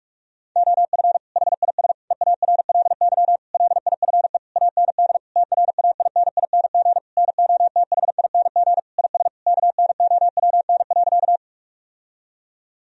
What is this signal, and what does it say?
Morse code, heard as a keyed tone, about 47 wpm, 700 Hz. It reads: OF HIS EARLY LIFE AND TRAINING NOTHING IS KNOWN.
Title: Sentences from Top 1000 Words - Review Part 4 - 45wpm